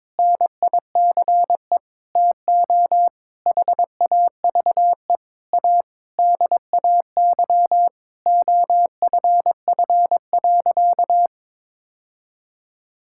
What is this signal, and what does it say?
Morse code, heard as a keyed tone, about 22 wpm, 700 Hz. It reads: NICE TO HAVE A DAY OFF.
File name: Sentences with 7-Word Phrases - 22wpm